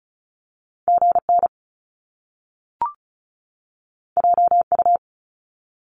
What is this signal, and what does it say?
Morse code, heard as a keyed tone, about 36 wpm, 700 Hz. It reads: GN JU
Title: Top Two Letter Combinations in Words 35wpm